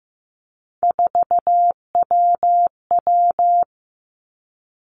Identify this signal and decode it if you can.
Morse code, heard as a keyed tone, about 15 wpm, 700 Hz. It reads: 4WW
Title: Three Challenging Letter-Number Combinations 15wpm